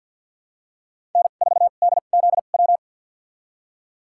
Morse code, heard as a keyed tone, about 47 wpm, spaced as for 35 wpm, 700 Hz. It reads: N4DZW
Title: USA Call Signs - 2x Character Spacing - 50wpm